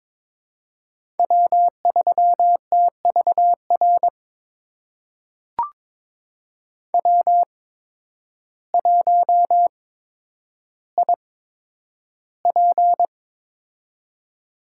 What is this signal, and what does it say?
Morse code, heard as a keyed tone, about 22 wpm, spaced as for 10 wpm, 700 Hz. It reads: W3TVR W1IP
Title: USA Call Signs - 8x Character Spacing - 22wpm